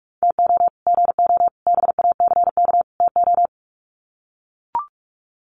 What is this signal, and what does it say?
Morse code, heard as a keyed tone, about 44 wpm, 700 Hz. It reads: TO GO BACK TO
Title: Sets of 4 Words 45wpm